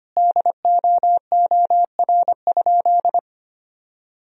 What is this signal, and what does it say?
Morse code, heard as a keyed tone, about 25 wpm, 700 Hz. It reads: DOOR?
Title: Sentences from Top 1000 Words - Review Part 2 - 25wpm